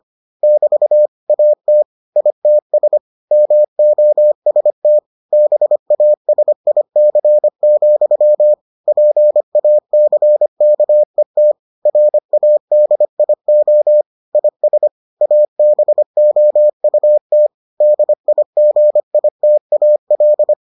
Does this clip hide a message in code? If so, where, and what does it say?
Yes: Morse code from 0.4 to 20.5 s, heard as a keyed tone, about 25 wpm, 600 Hz. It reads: <BT> AT ITS MOST BASIC, PACKET RADIO IS ABOUT DIGITAL